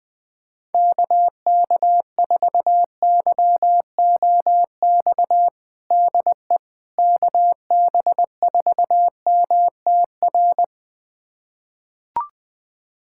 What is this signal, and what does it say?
Morse code, heard as a keyed tone, about 20 wpm, 700 Hz. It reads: KK4YOX DE KB4MTR
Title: USA Callsign DE Callsign 20wpm